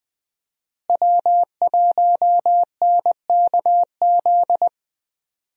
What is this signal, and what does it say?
Morse code, heard as a keyed tone, about 20 wpm, 700 Hz. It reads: W1NKZ
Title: USA Call Signs - 5 Characters - Repeated 3x - 20wpm